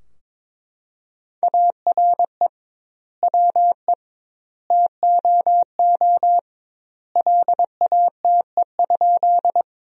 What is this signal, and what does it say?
Morse code, heard as a keyed tone, about 21 wpm, 700 Hz. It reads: ARE WE TOO LATE?
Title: 2x Word-Spacing - Sentences from Top 300 Words - 22wpm